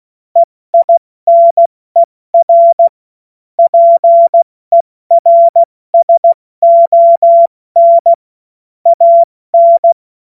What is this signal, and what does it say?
Morse code, heard as a keyed tone, about 16 wpm, 680 Hz. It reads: EINER PERSON AN
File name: Morsekurs / Witze / Witz 03. Mai 2022 70 F80